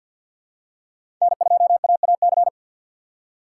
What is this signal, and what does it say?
Morse code, heard as a keyed tone, about 53 wpm, 700 Hz. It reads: N2AAC
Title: USA Call Signs - 4x Character Spacing - 50wpm